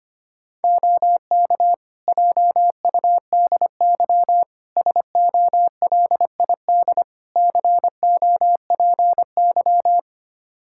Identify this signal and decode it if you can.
Morse code, heard as a keyed tone, about 25 wpm, 700 Hz. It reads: OK JUDY SOLID COPY
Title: Practice QSO - Exchange 3 - More Info - 25wpm